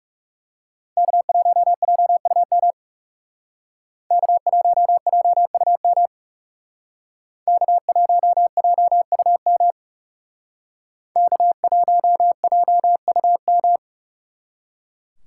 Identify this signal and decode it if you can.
Morse code, heard as a keyed tone, about 38 wpm, 700 Hz. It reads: K1JUM K1JUM K1JUM K1JUM